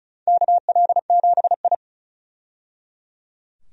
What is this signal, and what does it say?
Morse code, heard as a keyed tone, about 36 wpm, 700 Hz. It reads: KL7I
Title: DXCC-100 Call Signs - 4 Characters - 35wpm